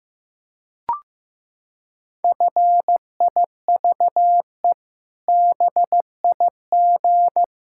Morse code, heard as a keyed tone, about 16 wpm, 700 Hz. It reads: FIVE BIG
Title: Pangram Challenge v2 - 15wpm